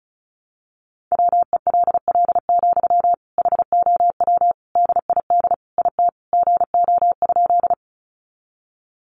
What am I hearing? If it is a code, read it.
Morse code, heard as a keyed tone, about 34 wpm, 700 Hz. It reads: WELL, HOW DID IT GO?
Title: Sentences from Top 200 Words 35wpm